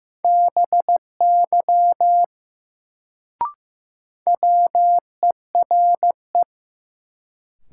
Morse code, heard as a keyed tone, about 15 wpm, 700 Hz. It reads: BY WERE
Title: Sentences from Top 100 Words - New Word Drill - 15wpm